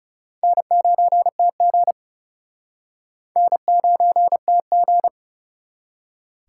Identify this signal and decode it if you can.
Morse code, heard as a keyed tone, about 37 wpm, 700 Hz. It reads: N9TG N9TG
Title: Speed Racing - USA 1x2 Call Signs 30wpm